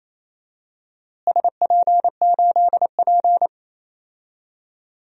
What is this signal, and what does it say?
Morse code, heard as a keyed tone, about 28 wpm, 700 Hz. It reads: SP8P
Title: DXCC-100 Call Signs - 4 Characters - Repeated 2x - 28wpm